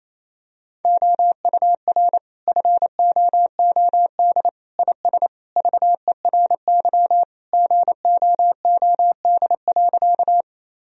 Morse code, heard as a keyed tone, about 28 wpm, 700 Hz. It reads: OUR FOOD IS VERY GOOD.
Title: Sentences from Top 1000 Words - Review Part 1 - 28wpm